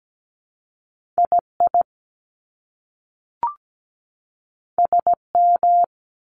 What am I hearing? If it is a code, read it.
Morse code, heard as a keyed tone, about 17 wpm, spaced as for 3 wpm, 700 Hz. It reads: I IS M
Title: Top Two Letter Combinations in Words 17wpm